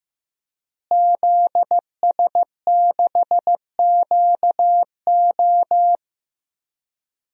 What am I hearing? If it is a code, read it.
Morse code, heard as a keyed tone, about 15 wpm, 700 Hz. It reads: ZS6QO
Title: DXCC-100 Call Signs - 5 Characters - Repeated 3x - 15wpm